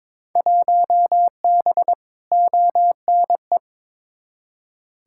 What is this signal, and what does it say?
Morse code, heard as a keyed tone, about 22 wpm, 700 Hz. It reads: 1B ONE
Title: Field Day Exchanges - 22wpm